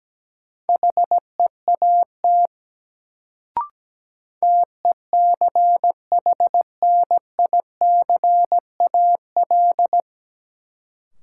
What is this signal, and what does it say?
Morse code, heard as a keyed tone, about 17 wpm, 700 Hz. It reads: HEAT TECHNICAL
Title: Sentences from Top 1100 Words - New Word Drill Alt - 17wpm